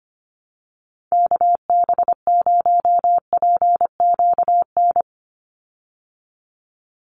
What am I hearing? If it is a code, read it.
Morse code, heard as a keyed tone, about 25 wpm, 700 Hz. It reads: KB0PQN
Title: USA Call Signs 25wpm